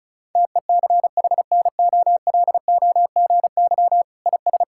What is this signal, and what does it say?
Morse code, heard as a keyed tone, about 36 wpm, 700 Hz. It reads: TECHNOLOGY IS
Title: Sentences from Top 1000 Words - Review Part 5 - 35wpm